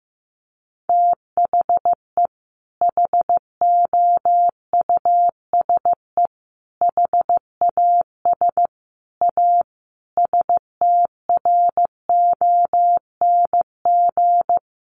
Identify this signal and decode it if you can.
Morse code, heard as a keyed tone, about 15 wpm, 700 Hz. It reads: THE HOUSE HAS A STRONG
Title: Sentences from Top 1200 Words 15wpm